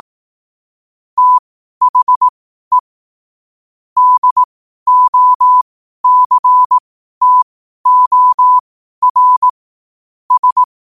Morse code, heard as a keyed tone, about 18 wpm, spaced as for 12 wpm, 1000 Hz. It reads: THE DOCTOR S
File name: qotdfiles / 12wpm